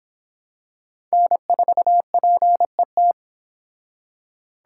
Morse code, heard as a keyed tone, about 26 wpm, 700 Hz. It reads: N4PET